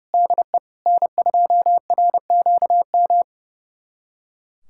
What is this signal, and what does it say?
Morse code, heard as a keyed tone, about 30 wpm, 700 Hz. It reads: DE N2RQM
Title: USA Callsign DE Callsign - 5 Characters - 30wpm